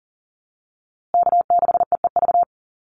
Morse code, heard as a keyed tone, about 40 wpm, 700 Hz. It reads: K6EEV
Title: USA Call Signs Repeated 2x 40wpm